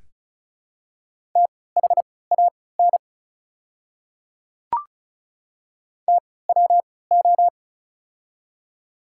Morse code, heard as a keyed tone, about 34 wpm, spaced as for 20 wpm, 700 Hz. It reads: THAN TWO
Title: 3x Character Spacing - Top 500 Words - 35wpm